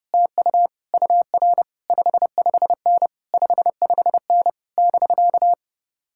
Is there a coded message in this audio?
Yes: Morse code from 0.1 to 5.5 s, heard as a keyed tone, about 33 wpm, 700 Hz. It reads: TU UR 55N 55N <BK>